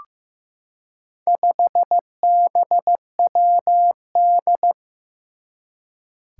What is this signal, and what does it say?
Morse code, heard as a keyed tone, about 15 wpm, 700 Hz. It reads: 5BWD
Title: Four Challenging Letter-Number Combinations 15wpm